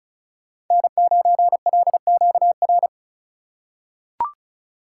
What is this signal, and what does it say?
Morse code, heard as a keyed tone, about 34 wpm, 700 Hz. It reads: N9LQR
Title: USA Call Signs - 3x Character Spacing - 35wpm